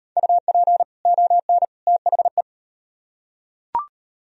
Morse code, heard as a keyed tone, about 38 wpm, 700 Hz. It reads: UP ON THE